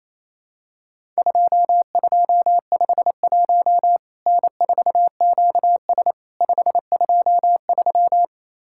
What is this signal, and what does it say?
Morse code, heard as a keyed tone, about 28 wpm, 700 Hz. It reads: 2251 N4QS 523
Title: Worked All Europe DX Contest - QTC Exchange - 28wpm